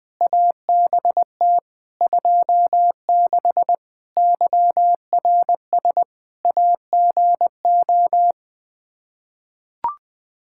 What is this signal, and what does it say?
Morse code, heard as a keyed tone, about 21 wpm, 700 Hz. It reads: ABT 26 YRS AGO
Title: Practice QSO - Exchange 3 - More Info - 20wpm